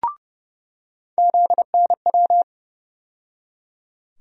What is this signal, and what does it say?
Morse code, heard as a keyed tone, about 30 wpm, 700 Hz. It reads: ZNW
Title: USA Call Sign Postfixes - 3 Letters 30wpm